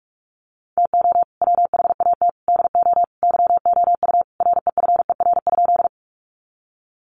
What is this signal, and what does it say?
Morse code, heard as a keyed tone, about 44 wpm, 700 Hz. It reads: TO WHAT DO YOU REFER?
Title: Sentences from Top 1000 Words 45wpm